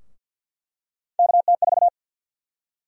Morse code, heard as a keyed tone, about 53 wpm, 700 Hz. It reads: KT4
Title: USA Call Sign Prefixes - 50wpm